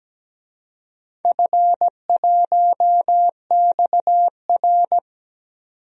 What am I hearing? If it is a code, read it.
Morse code, heard as a keyed tone, about 17 wpm, 700 Hz. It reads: F1XR
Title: DXCC-100 Call Signs - 4 Characters - 17wpm